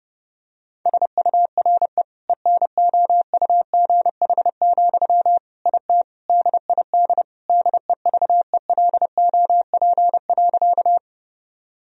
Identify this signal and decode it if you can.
Morse code, heard as a keyed tone, about 30 wpm, 700 Hz. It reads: SURE ENOUGH, IT DID DEVELOP.
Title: Sentences from Top 1000 Words - Review Part 2 - 30wpm